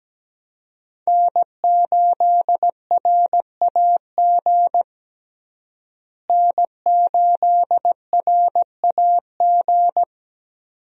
Morse code, heard as a keyed tone, about 17 wpm, 700 Hz. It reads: N8RAG N8RAG
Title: USA Call Signs - 5 Characters - Repeated 2x - 17wpm